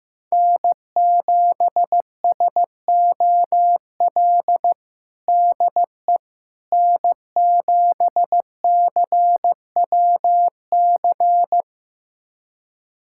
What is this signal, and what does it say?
Morse code, heard as a keyed tone, about 16 wpm, 700 Hz. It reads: N7SOL DE N7CWC
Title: USA Callsign DE Callsign - 5 Characters - 15wpm